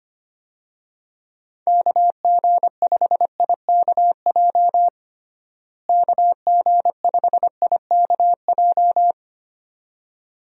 Morse code, heard as a keyed tone, about 25 wpm, 700 Hz. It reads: KG5IKJ KG5IKJ
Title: USA Call Signs - 6 Characters - Repeated 2x - 25wpm